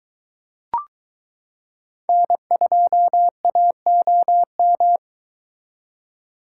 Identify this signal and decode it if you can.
Morse code, heard as a keyed tone, about 23 wpm, 700 Hz. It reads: N2AOM